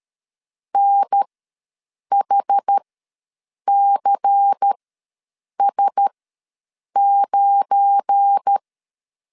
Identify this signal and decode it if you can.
Morse code, heard as a keyed tone, about 13 wpm, spaced as for 7 wpm, 800 Hz. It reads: NHCS9
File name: LEZIONI / 10 SIGNS / 10A SLASH ws 12wpm 3443